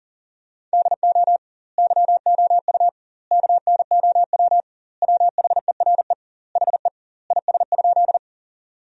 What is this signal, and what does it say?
Morse code, heard as a keyed tone, about 40 wpm, 700 Hz. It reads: DO YOU KNOW WHERE HE IS?
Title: 2x Word-Spacing - Sentences from Top 300 Words - 40wpm